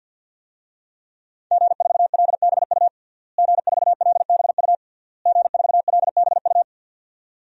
Morse code, heard as a keyed tone, about 52 wpm, 700 Hz. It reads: G4LBU G4LBU G4LBU
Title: DXCC-100 Call Signs - 5 Characters - Repeated 3x - 50wpm